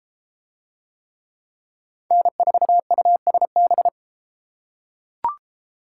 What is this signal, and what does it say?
Morse code, heard as a keyed tone, about 33 wpm, 700 Hz. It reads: N4USB